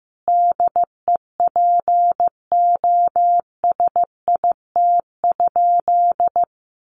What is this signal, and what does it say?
Morse code, heard as a keyed tone, about 15 wpm, 700 Hz. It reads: DEPOSIT?
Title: Sentences from Top 1300 Words 15wpm